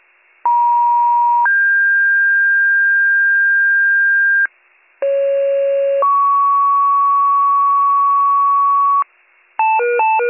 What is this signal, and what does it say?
Two-tone page, heard as a one-second tone, then a three-second tone, about 950 then 1600 Hz, about 550 then 1120 Hz, with noise. It is heard from 0.4 to 9.0 s.